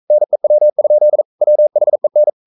Morse code, heard as a keyed tone, about 42 wpm, 600 Hz. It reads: NEW? WHEN
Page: Romanen Never af Ken Follet på engelsk.